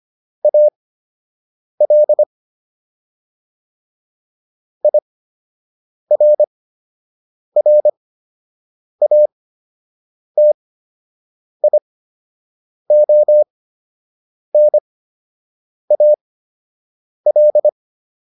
Morse code, heard as a keyed tone, about 25 wpm, spaced as for 7 wpm, 600 Hz. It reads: AL IRRATIONAL